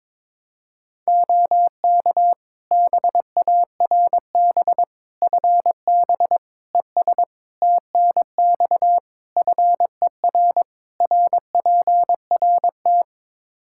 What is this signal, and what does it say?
Morse code, heard as a keyed tone, about 22 wpm, 700 Hz. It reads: OK BARB FB ES TNX FER RPRT
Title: Practice QSO - Exchange 2 - Info - 22wpm